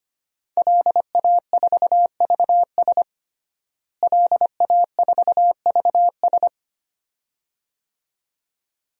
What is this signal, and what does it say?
Morse code, heard as a keyed tone, about 25 wpm, 700 Hz. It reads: LA4VS LA4VS
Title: DXCC-100 Call Signs - 5 Characters - Repeated 2x - 25wpm